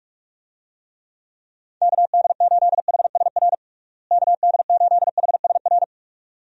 Morse code, heard as a keyed tone, about 46 wpm, 700 Hz. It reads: KD8HSR KD8HSR
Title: USA Call Signs - 6 Characters - Repeated 2x - 45wpm